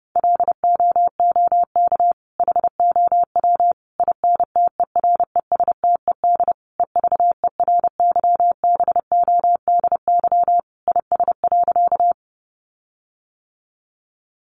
Morse code, heard as a keyed tone, about 30 wpm, 700 Hz. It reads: LOOK HOW INTERESTED EVERYBODY IS.
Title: Sentences from Top 1400 Words 30wpm